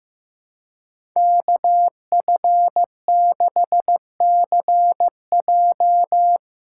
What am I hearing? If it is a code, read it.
Morse code, heard as a keyed tone, about 15 wpm, 700 Hz. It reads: KF6CJ
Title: USA Call Signs - 5 Characters - 3x Character Spacing - 15wpm